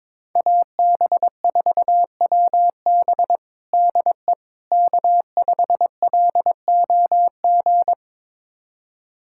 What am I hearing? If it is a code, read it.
Morse code, heard as a keyed tone, about 21 wpm, 700 Hz. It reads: AB4WB DE K5LOG
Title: USA Callsign DE Callsign - 5 Characters - 22wpm